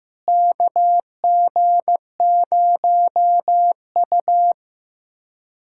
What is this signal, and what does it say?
Morse code, heard as a keyed tone, about 16 wpm, 700 Hz. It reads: KG0U
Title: USA Callsign DE Callsign - 4 Characters - 15wpm